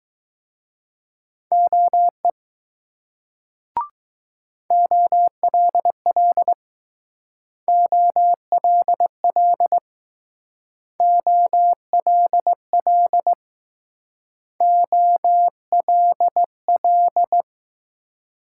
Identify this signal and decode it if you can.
Morse code, heard as a keyed tone, about 20 wpm, 700 Hz. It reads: OE OLL OLL OLL OLL